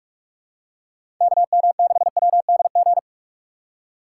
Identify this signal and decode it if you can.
Morse code, heard as a keyed tone, about 46 wpm, 700 Hz. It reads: KM6WDG
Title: USA Call Signs - 6 Characters - 3x Character Spacing - 45wpm